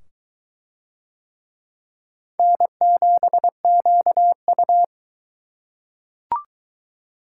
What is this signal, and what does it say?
Morse code, heard as a keyed tone, about 22 wpm, 700 Hz. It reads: N7QU